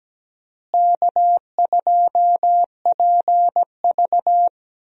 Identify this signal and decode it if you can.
Morse code, heard as a keyed tone, about 17 wpm, 700 Hz. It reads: K2PV
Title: USA Call Signs - 4 Characters - Repeated 3x - 17wpm